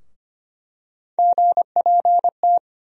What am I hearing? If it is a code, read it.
Morse code, heard as a keyed tone, about 25 wpm, 700 Hz. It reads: GPT
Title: USA Call Sign Postfixes - 3 Letters 25wpm